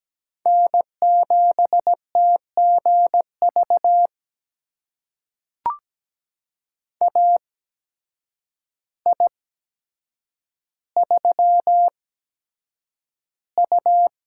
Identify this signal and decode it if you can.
Morse code, heard as a keyed tone, about 17 wpm, spaced as for 8 wpm, 700 Hz. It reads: N7TGV AI3U
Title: USA Call Signs - 8x Character Spacing - 17wpm